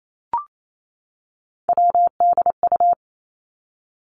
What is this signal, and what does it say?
Morse code, heard as a keyed tone, about 28 wpm, 700 Hz. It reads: WDU
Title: Three Challenging Letter-Number Combinations 28wpm